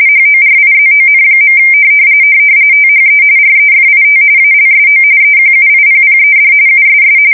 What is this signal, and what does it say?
RTTY, about 45.45 baud. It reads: HORIZON. THE REFIT ALSO EQUIPPED USS
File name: fsk170 / USS-New-Jersey